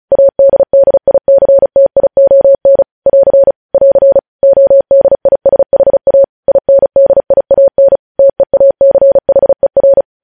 Morse code, heard as a keyed tone, about 36 wpm, 550 Hz. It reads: ADDICTION <AR> <AR> ODISHA INDIAN TEACHER
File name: podcast / News headlines-1x-35x35